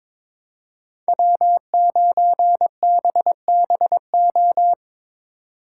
Morse code, heard as a keyed tone, about 22 wpm, 700 Hz. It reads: W9BBO
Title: USA Call Signs - 5 Characters - 3x Character Spacing - 22wpm